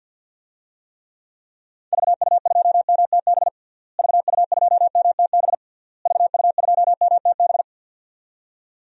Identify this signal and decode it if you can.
Morse code, heard as a keyed tone, about 50 wpm, 700 Hz. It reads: VU2MTB VU2MTB VU2MTB
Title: DXCC-100 Call Signs - 6 Characters - Repeated 3x - 50wpm